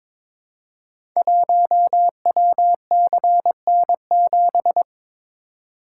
Morse code, heard as a keyed tone, about 22 wpm, 700 Hz. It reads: 1WCN7
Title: Code Groups - 5 Letter:Numbers - 22wpm